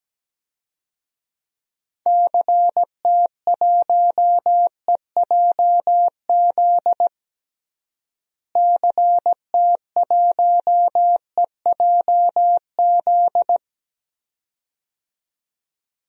Morse code, heard as a keyed tone, about 17 wpm, 700 Hz. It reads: CT1EJZ CT1EJZ
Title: DXCC-100 Call Signs - 6 Characters - Repeated 2x - 17wpm